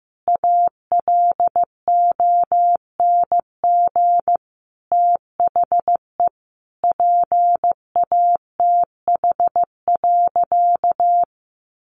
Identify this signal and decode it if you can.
Morse code, heard as a keyed tone, about 15 wpm, 700 Hz. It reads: ALONG THE PATH.
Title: A Sound Of Thunder 15wpm